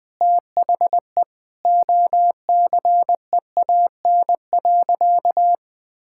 Morse code, heard as a keyed tone, about 19 wpm, 700 Hz. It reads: THE OCEAN.
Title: Sentences with 8-Word Phrases - 20wpm